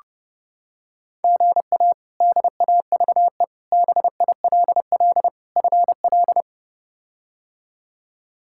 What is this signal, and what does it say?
Morse code, heard as a keyed tone, about 28 wpm, 700 Hz. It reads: GA DAVE BILL FL
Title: K1USN SST QSO Practice - 30wpm